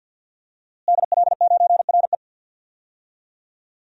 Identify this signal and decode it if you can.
Morse code, heard as a keyed tone, about 52 wpm, 700 Hz. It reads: DL9RE
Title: DXCC-100 Call Signs - 5 Characters - Repeated 2x - 50wpm